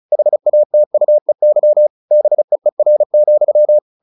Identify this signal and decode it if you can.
Morse code, heard as a keyed tone, about 34 wpm, 600 Hz. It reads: HATUEY BEER,
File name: the old man and the sea / the old man and the sea 1 35wpm